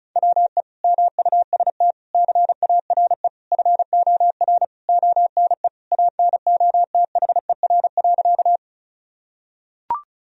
Morse code, heard as a keyed tone, about 36 wpm, 700 Hz. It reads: WE MUST CARE FOR ONE ANOTHER.
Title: Sentences from Top 1000 Words - Review Part 1 - 35wpm